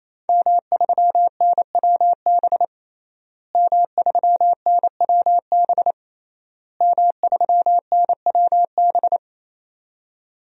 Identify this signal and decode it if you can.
Morse code, heard as a keyed tone, about 28 wpm, 700 Hz. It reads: M3NWB M3NWB M3NWB
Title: DXCC-100 Call Signs - 5 Characters - Repeated 3x - 28wpm